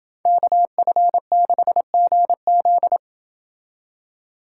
Morse code, heard as a keyed tone, about 28 wpm, 700 Hz. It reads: KF6GZ